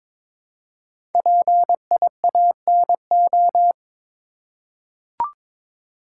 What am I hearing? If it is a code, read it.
Morse code, heard as a keyed tone, about 22 wpm, 700 Hz. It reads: PIANO
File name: 5 Letter Words - First 3 Letters Given - 22wpm